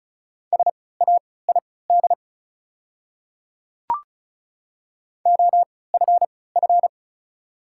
Morse code, heard as a keyed tone, about 36 wpm, spaced as for 20 wpm, 690 Hz. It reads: SAID OFF
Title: 3x Character Spacing - Top 200 Words - 35wpm